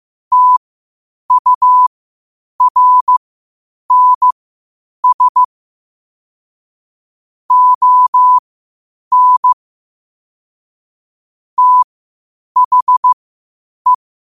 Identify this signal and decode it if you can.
Morse code, heard as a keyed tone, about 15 wpm, spaced as for 8 wpm, 1000 Hz. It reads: TURNS ON THE
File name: qotdfiles / 8wpm